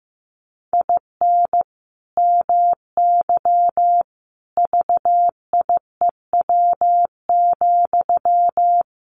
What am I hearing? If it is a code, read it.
Morse code, heard as a keyed tone, about 15 wpm, 700 Hz. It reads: IN MY VIEW,
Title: Sentences from Top 1500 Words 15wpm